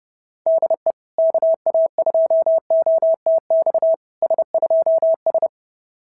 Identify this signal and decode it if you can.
Morse code, heard as a keyed tone, about 30 wpm, 660 Hz. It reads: DE KA2OTX S2S
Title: SOTA QSO Practice 30wpm